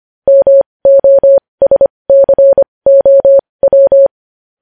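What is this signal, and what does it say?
Morse code, heard as a keyed tone, about 25 wpm, spaced as for 20 wpm, 550 Hz. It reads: MOSCOW
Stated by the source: Morse code news headlines sent at 20 wpm.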